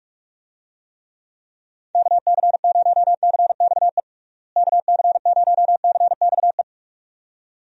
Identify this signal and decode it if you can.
Morse code, heard as a keyed tone, about 46 wpm, 700 Hz. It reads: KC0CXE KC0CXE
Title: USA Call Signs - 6 Characters - Repeated 2x - 45wpm